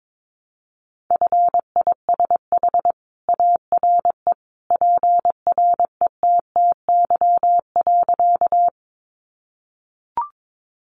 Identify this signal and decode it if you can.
Morse code, heard as a keyed tone, about 22 wpm, 700 Hz. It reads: FISH ARE PRETTY.
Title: Additional Easy Phrases 22wpm